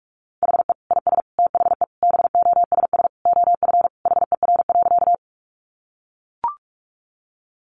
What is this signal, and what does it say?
Morse code, heard as a keyed tone, about 44 wpm, 700 Hz. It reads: HE IS THE BOSS OF HER.
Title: Sentences from Top 700 Words 45wpm